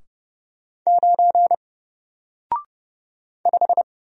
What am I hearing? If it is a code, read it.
Morse code, heard as a keyed tone, about 30 wpm, 700 Hz. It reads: 9 5
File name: Single-Letter-Number-30wpm